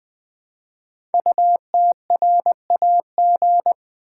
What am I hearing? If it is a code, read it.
Morse code, heard as a keyed tone, about 20 wpm, 700 Hz. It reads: UTRAG
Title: Code Groups - 5 Letter:Numbers - 20wpm